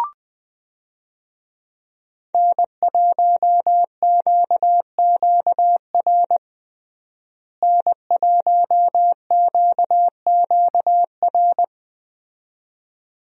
Morse code, heard as a keyed tone, about 20 wpm, 700 Hz. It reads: N1QQR N1QQR
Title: USA Call Signs - 5 Characters - Repeated 2x - 20wpm